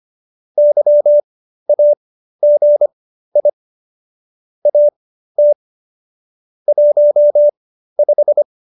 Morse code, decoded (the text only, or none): YAGI AT 15